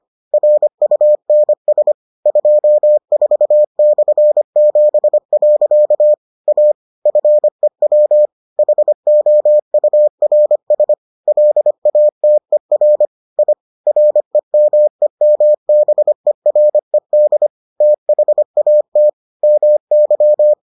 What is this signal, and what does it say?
Morse code, heard as a keyed tone, about 25 wpm, 600 Hz. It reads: RUNS 24/7. A FEW HOURS LATER I REMEMBERED THAT MY